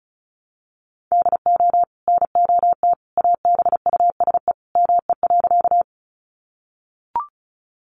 Morse code, heard as a keyed tone, about 34 wpm, 700 Hz. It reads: DO NOT ABUSE ME.
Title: Sentences from Top 1300 Words 35wpm